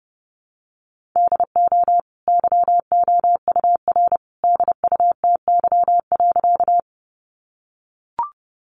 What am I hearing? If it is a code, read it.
Morse code, heard as a keyed tone, about 30 wpm, 700 Hz. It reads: DO YOUR DUTY.
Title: Sentences from Top 1500 Words 30wpm